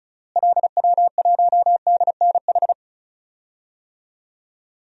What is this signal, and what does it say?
Morse code, heard as a keyed tone, about 34 wpm, 700 Hz. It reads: LW1DNH
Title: DXCC-100 Call Signs - 6 Characters - Repeated 2x - 35wpm